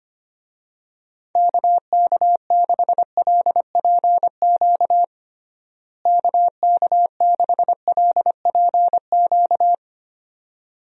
Morse code, heard as a keyed tone, about 25 wpm, 700 Hz. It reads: KK6LPQ KK6LPQ
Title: USA Call Signs - 6 Characters - Repeated 2x - 25wpm